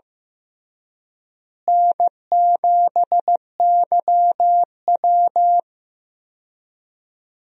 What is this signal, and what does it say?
Morse code, heard as a keyed tone, about 15 wpm, 700 Hz. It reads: N7YW
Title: USA Call Signs - 4 Characters - Repeated 3x - 15wpm